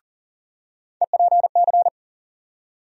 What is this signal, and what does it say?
Morse code, heard as a keyed tone, about 40 wpm, 700 Hz. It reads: EPC
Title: USA Call Sign Postfixes - 3 Letters 40wpm